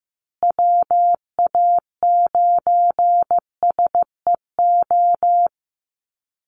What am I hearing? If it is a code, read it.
Morse code, heard as a keyed tone, about 15 wpm, 700 Hz. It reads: WA9SEO
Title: Call Signs Repeated 3x - 15wpm